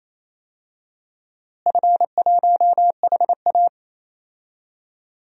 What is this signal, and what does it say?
Morse code, heard as a keyed tone, about 28 wpm, 700 Hz. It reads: F1HA
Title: DXCC-100 Call Signs - 4 Characters - Repeated 2x - 28wpm